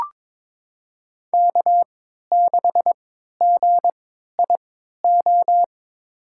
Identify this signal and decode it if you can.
Morse code, heard as a keyed tone, about 22 wpm, spaced as for 12 wpm, 700 Hz. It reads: K6GIO
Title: USA Call Signs - 3x Character Spacing - 22wpm